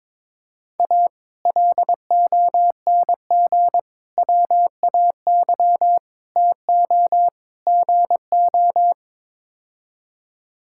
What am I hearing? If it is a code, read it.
Morse code, heard as a keyed tone, about 22 wpm, 700 Hz. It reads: A LONG WAY TO GO
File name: practice / 22wpm / Build A Phrase - 5-Word Phrases - 22wpm